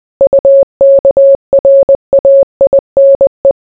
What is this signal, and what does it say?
Morse code, heard as a keyed tone, about 20 wpm, 550 Hz. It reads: UKRAINE